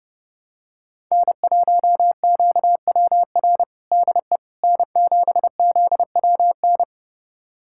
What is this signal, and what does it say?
Morse code, heard as a keyed tone, about 30 wpm, 700 Hz. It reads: N1QWR DE N7ZWN
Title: USA Callsign DE Callsign - 5 Characters - 30wpm